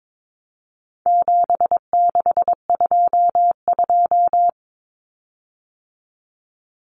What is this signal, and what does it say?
Morse code, heard as a keyed tone, about 22 wpm, 700 Hz. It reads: 7622
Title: Common Cut Numbers - 4-Digit 22wpm